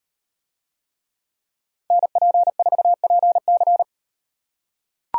Morse code, heard as a keyed tone, about 38 wpm, 700 Hz. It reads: NP4PC